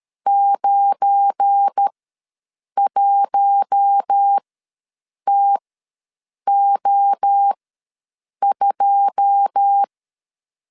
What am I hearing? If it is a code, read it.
Morse code, heard as a keyed tone, about 13 wpm, spaced as for 7 wpm, 800 Hz. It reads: 91TO2
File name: LEZIONI / 9 2 ws 12wpm 3406